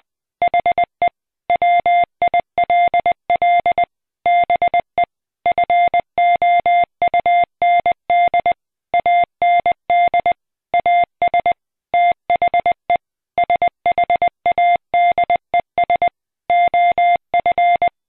Morse code, heard as a keyed tone, about 20 wpm, 700 Hz. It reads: HE WILL BE FOUND AND AS THE SHADES OF